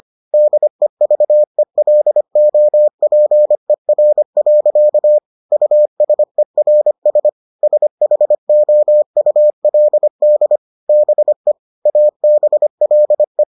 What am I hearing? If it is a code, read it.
Morse code, heard as a keyed tone, about 25 wpm, 600 Hz. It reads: DEVELOPER. USERS SHOULD BE ABLE